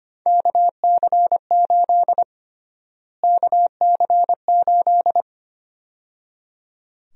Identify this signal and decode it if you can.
Morse code, heard as a keyed tone, about 25 wpm, 700 Hz. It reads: KC8 KC8
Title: USA Call Sign Prefixes - Repeated 2x - 25wpm